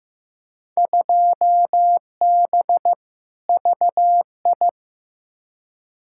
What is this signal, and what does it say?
Morse code, heard as a keyed tone, about 15 wpm, 700 Hz. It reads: 2B VI
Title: Field Day Exchanges Encore - 15wpm